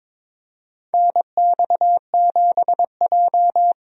Morse code, heard as a keyed tone, about 22 wpm, 700 Hz. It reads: NX7J
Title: USA Call Signs - 4 Characters - 22wpm